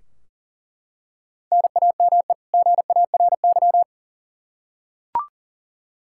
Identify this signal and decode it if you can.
Morse code, heard as a keyed tone, about 40 wpm, 700 Hz. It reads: NAME GARY
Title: Practice QSO - Exchange 1 - Beginning - 40wpm